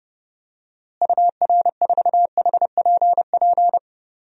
Morse code, heard as a keyed tone, about 29 wpm, 700 Hz. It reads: UR4HPP
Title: DXCC-100 Call Signs - 6 Characters - Repeated 3x - 30wpm